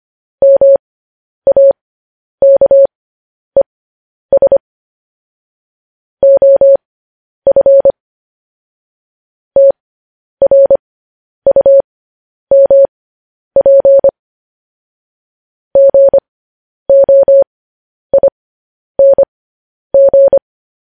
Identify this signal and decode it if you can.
Morse code, heard as a keyed tone, about 25 wpm, spaced as for 10 wpm, 550 Hz. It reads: MAKES OF TRUMP GOING